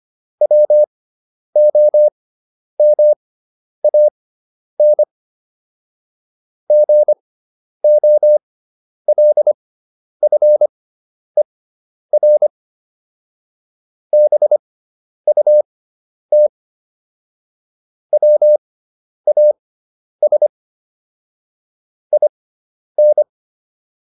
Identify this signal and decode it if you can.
Morse code, heard as a keyed tone, about 25 wpm, spaced as for 10 wpm, 600 Hz. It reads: WOMAN GOLFER BUT WAS IN